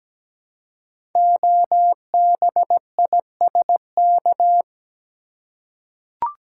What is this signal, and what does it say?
Morse code, heard as a keyed tone, about 17 wpm, 700 Hz. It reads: OBISK
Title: Code Groups - 5 Letters - 17wpm